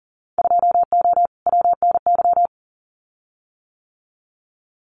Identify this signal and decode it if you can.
Morse code, heard as a keyed tone, about 40 wpm, 700 Hz. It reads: 2O WNY
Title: Winter Field Day Exchanges Encore 40wpm